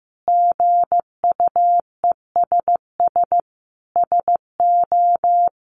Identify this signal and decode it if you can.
Morse code, heard as a keyed tone, about 15 wpm, 700 Hz. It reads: GUESS SO
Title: Sentences from Top 1500 Words 15wpm